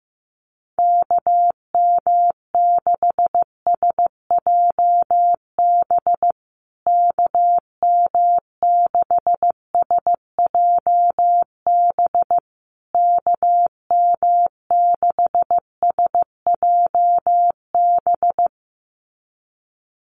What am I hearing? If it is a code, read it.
Morse code, heard as a keyed tone, about 15 wpm, 700 Hz. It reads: KM6SJB KM6SJB KM6SJB
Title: USA Call Signs Repeated 3x 15wpm